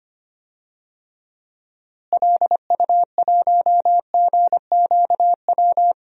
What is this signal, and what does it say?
Morse code, heard as a keyed tone, about 25 wpm, 700 Hz. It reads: LU1GQW
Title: DXCC-100 Call Signs - 6 Characters - Repeated 2x - 25wpm